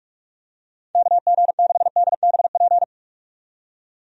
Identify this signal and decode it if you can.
Morse code, heard as a keyed tone, about 46 wpm, 700 Hz. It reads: KG6DBP
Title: USA Call Signs - 6 Characters - 45wpm